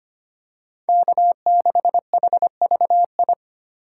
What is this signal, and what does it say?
Morse code, heard as a keyed tone, about 25 wpm, 700 Hz. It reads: K6HVI
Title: USA Call Signs - 5 Characters - 3x Character Spacing - 25wpm